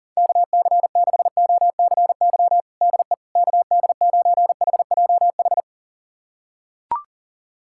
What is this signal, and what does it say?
Morse code, heard as a keyed tone, about 40 wpm, 700 Hz. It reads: KC6OCY DE KD9HJH
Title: USA Callsign DE Callsign - 6 Characters - 40wpm